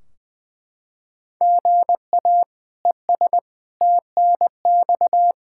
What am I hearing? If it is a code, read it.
Morse code, heard as a keyed tone, about 20 wpm, 700 Hz. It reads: GA ES TNX
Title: Practice QSO - Exchange 1 - Beginning - 20wpm